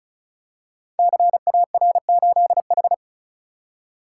Morse code, heard as a keyed tone, about 36 wpm, 700 Hz. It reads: CAR8H
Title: Code Groups - 5 Letter:Numbers - 35wpm